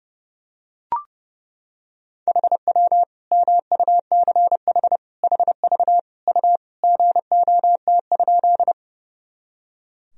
Morse code, heard as a keyed tone, about 30 wpm, 700 Hz. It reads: HW MUCH HV U GOT?
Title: Sentences from Top 300 Words and Abbreviations 30wpm